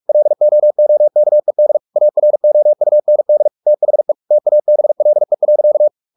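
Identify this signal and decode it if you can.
Morse code, heard as a keyed tone, about 44 wpm, 600 Hz. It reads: LOOKED AROUND THE TABLE.
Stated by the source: Morse practice at 45 wpm, the novel Never by Ken Follett in English, from CW med Gnister.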